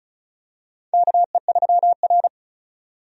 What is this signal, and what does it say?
Morse code, heard as a keyed tone, about 36 wpm, 700 Hz. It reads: KE3R
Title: USA Call Signs - 4 Characters - 35wpm